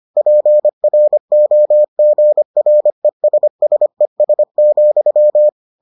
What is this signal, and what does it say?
Morse code, heard as a keyed tone, about 25 wpm, 600 Hz. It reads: PROGRESSES,